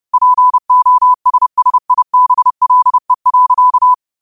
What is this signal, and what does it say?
Morse code, heard as a keyed tone, about 30 wpm, 1000 Hz. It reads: POSSIBLE.